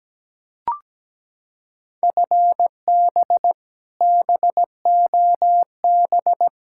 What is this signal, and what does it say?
Morse code, heard as a keyed tone, about 17 wpm, 700 Hz. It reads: FB BOB
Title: Practice QSO - Exchange 3 - More Info - 17wpm